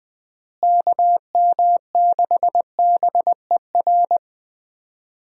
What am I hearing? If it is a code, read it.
Morse code, heard as a keyed tone, about 20 wpm, 700 Hz. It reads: KM6BER
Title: USA Call Signs - 6 Characters - Repeated 3x - 20wpm